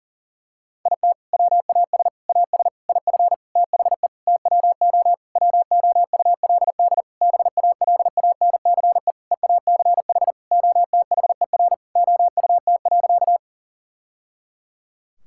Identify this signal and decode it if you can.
Morse code, heard as a keyed tone, about 40 wpm, 700 Hz. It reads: IT WAS AS IF THE TWO WOULD BALANCE EACH OTHER OUT.
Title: Sentences from Top 1000 Words - Review Part 6 - 40wpm